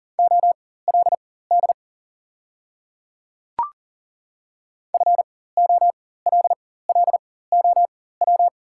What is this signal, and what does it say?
Morse code, heard as a keyed tone, about 40 wpm, spaced as for 19 wpm, 700 Hz. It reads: OLD FOLLOW
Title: 4x Character Spacing - Top 300 Words - 40wpm